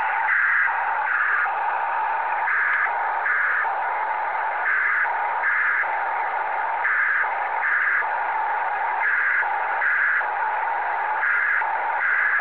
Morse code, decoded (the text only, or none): none